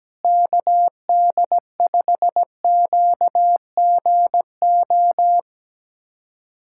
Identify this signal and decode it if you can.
Morse code, heard as a keyed tone, about 17 wpm, 700 Hz. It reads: KD5QGO
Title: USA Callsign DE Callsign 17wpm